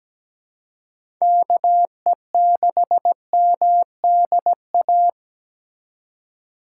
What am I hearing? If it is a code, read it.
Morse code, heard as a keyed tone, about 17 wpm, 700 Hz. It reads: KE6MDA
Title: USA Call Signs - 2x Character Spacing - 17wpm